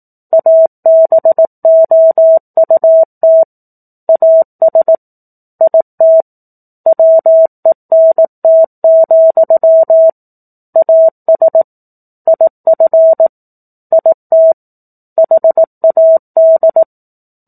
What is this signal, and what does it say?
Morse code, heard as a keyed tone, about 18 wpm, 650 Hz. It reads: ABOUT AS IT WENT, AS IF IT HAD
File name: 04-Chapter-04 18 WPM